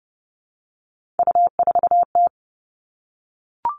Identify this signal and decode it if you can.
Morse code, heard as a keyed tone, about 30 wpm, 700 Hz. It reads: U4T
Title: All Cut Numbers - 3-Digit 30wpm